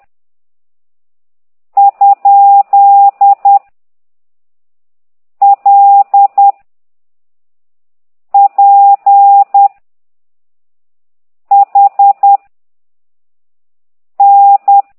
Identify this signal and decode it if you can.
Morse code, heard as a keyed tone, about 10 wpm, 800 Hz. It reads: ? L P H N
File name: ct2idw / exerc13